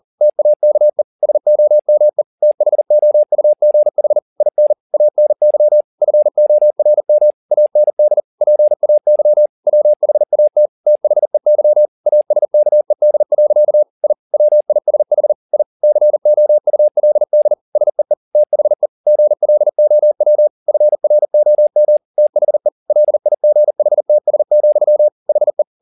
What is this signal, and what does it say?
Morse code, heard as a keyed tone, about 40 wpm, 600 Hz. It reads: TAKE SOME THOUGH IN ANY FORM AND PAY WHAT THEY ASKED. I WISH I COULD SEE THE GLOW FROM THE LIGHTS, HE
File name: the old man and the sea 2 40wpm